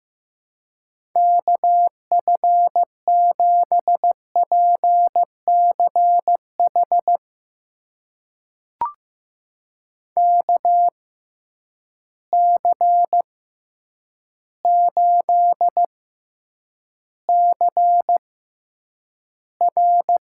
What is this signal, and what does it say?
Morse code, heard as a keyed tone, about 15 wpm, spaced as for 8 wpm, 700 Hz. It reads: KF7PCH KC8CR
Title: USA Call Signs - 6 Characters - 6x Character Spacing - 15wpm